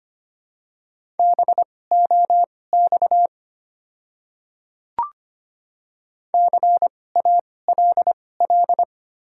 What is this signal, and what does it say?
Morse code, heard as a keyed tone, about 25 wpm, spaced as for 18 wpm, 700 Hz. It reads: BOX CALL
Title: 2x Character Spacing - Top 500 Words - 25wpm